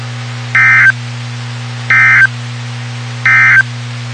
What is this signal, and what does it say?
Emergency Alert System (SAME) tones, heard as harsh buzzing bursts. It is heard from 0.4 to 3.7 s.
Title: WXK91 NOAA Weather Radio